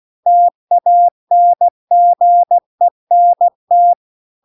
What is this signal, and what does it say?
Morse code, heard as a keyed tone, about 16 wpm, 700 Hz. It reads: TANGENT
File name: CW-The-Big-Twoheaded-River-2-16WPM